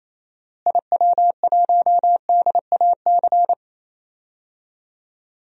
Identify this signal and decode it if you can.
Morse code, heard as a keyed tone, about 28 wpm, 700 Hz. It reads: IW1DAC
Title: DXCC-100 Call Signs - 6 Characters - Repeated 2x - 28wpm